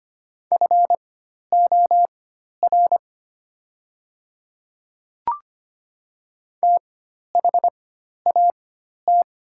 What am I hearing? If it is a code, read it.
Morse code, heard as a keyed tone, about 25 wpm, spaced as for 12 wpm, 700 Hz. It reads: FOR THAT
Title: 4x Character Spacing - Top 100 Words - 25wpm